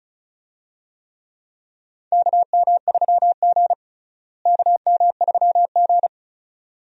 Morse code, heard as a keyed tone, about 34 wpm, 700 Hz. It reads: KM3G KM3G
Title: USA Call Signs - 4 Characters - Repeated 2x - 35wpm